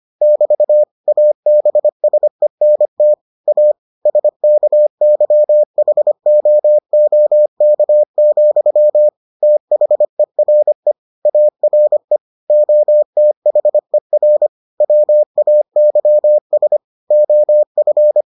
Morse code, heard as a keyed tone, about 25 wpm, 600 Hz. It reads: <BT> ABSENT A SKYHOOK, THERE ARE OTHER WAYS OF